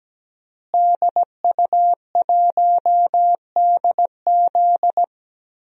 Morse code, heard as a keyed tone, about 17 wpm, 700 Hz. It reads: DU1DZ
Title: DXCC-100 Call Signs - 5 Characters - Repeated 3x - 17wpm